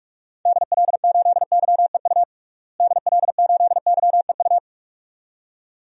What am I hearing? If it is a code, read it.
Morse code, heard as a keyed tone, about 46 wpm, 700 Hz. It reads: DL8YEU DL8YEU
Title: DXCC-100 Call Signs - 6 Characters - Repeated 2x - 45wpm